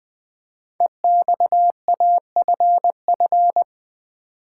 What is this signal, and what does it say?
Morse code, heard as a keyed tone, about 20 wpm, 700 Hz. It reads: EXAFF
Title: Code Groups - 5 Letters - 20wpm